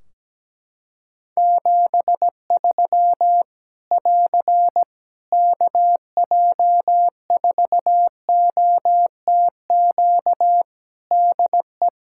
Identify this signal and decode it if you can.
Morse code, heard as a keyed tone, about 17 wpm, 700 Hz. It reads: 73 <AR> KJ4OTQ DE
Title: K1USN SST QSO Practice 17wpm